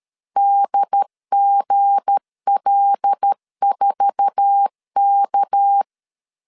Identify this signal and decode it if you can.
Morse code, heard as a keyed tone, about 13 wpm, 800 Hz. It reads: DGL4K
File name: LEZIONI / 8 YJ45 / 8 YJ4 12wpm 2151